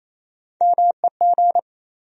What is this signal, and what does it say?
Morse code, heard as a keyed tone, about 28 wpm, 700 Hz. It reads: MEG
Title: USA Common Names - Max 3 Letters - 28wpm